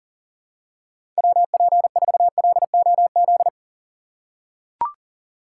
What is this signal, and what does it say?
Morse code, heard as a keyed tone, about 40 wpm, 700 Hz. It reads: WP4LOZ
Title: USA Call Signs - 6 Characters - 2x Character Spacing - 40wpm